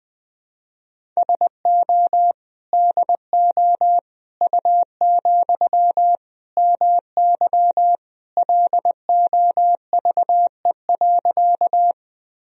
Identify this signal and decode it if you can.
Morse code, heard as a keyed tone, about 20 wpm, 700 Hz. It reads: SO DO U, MY LOVE.
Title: Sentences from Top 400 Words and Abbreviations 20wpm